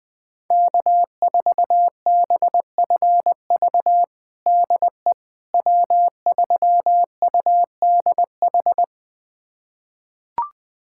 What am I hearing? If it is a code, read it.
Morse code, heard as a keyed tone, about 20 wpm, 700 Hz. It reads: K4BFV DE W3UDH
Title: USA Callsign DE Callsign - 5 Characters - 20wpm